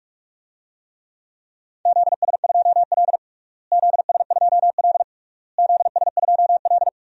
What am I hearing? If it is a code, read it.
Morse code, heard as a keyed tone, about 46 wpm, 700 Hz. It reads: ZS2L ZS2L ZS2L
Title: DXCC-100 Call Signs - 4 Characters - Repeated 3x - 45wpm